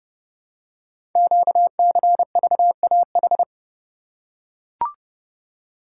Morse code, heard as a keyed tone, about 28 wpm, 700 Hz. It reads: QCVAH
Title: Code Groups - 5 Letters - 30wpm